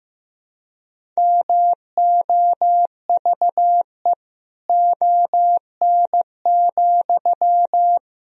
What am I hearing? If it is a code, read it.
Morse code, heard as a keyed tone, about 15 wpm, 700 Hz. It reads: MOVE ON,
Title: Sentences from Top 500 Words and Abbreviations 15wpm